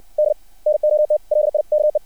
Morse code, heard as a keyed tone, about 60 wpm, 600 Hz. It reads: D NOT LIKE
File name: archived / CRSnet